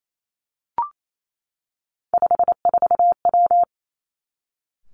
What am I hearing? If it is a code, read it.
Morse code, heard as a keyed tone, about 28 wpm, 700 Hz. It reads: 54W
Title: Three Challenging Letter-Number Combinations 28wpm